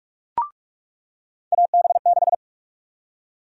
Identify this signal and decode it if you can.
Morse code, heard as a keyed tone, about 44 wpm, 700 Hz. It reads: AB6
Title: USA Call Sign Prefixes - 45wpm